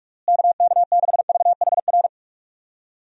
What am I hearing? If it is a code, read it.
Morse code, heard as a keyed tone, about 46 wpm, 700 Hz. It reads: KK6VHR
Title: USA Call Signs - 6 Characters - Repeated 3x - 45wpm